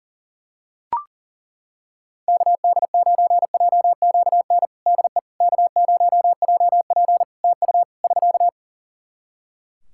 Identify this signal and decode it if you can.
Morse code, heard as a keyed tone, about 37 wpm, 700 Hz. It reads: KD9JQN DE K0JP TU <SK>
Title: Practice QSO - Exchange 4 - Ending - 40wpm